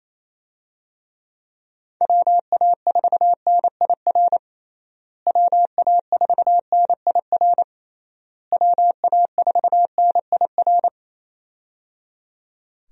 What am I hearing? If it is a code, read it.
Morse code, heard as a keyed tone, about 27 wpm, 700 Hz. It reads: WA4NIR WA4NIR WA4NIR
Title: USA Call Signs - 6 Characters - Repeated 3x - 28wpm